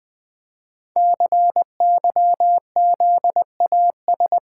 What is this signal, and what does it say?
Morse code, heard as a keyed tone, about 20 wpm, 700 Hz. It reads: CYZAS
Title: Code Groups - 5 Letters - 20wpm